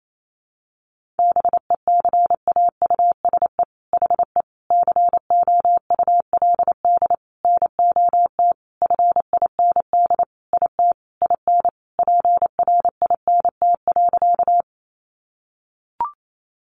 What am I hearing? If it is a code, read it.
Morse code, heard as a keyed tone, about 28 wpm, 700 Hz. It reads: BECAUSE HE COULD NOT FIND IT IN PRINT.
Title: Sentences from Top 1400 Words 28wpm